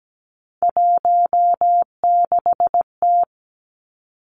Morse code, heard as a keyed tone, about 17 wpm, 700 Hz. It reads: 16T
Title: Three Challenging Letter-Number Combinations 17wpm